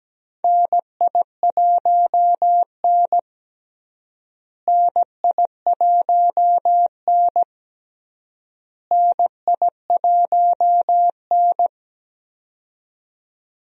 Morse code, heard as a keyed tone, about 17 wpm, 700 Hz. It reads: NI1N NI1N NI1N
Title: USA Call Signs - 4 Characters - Repeated 3x - 17wpm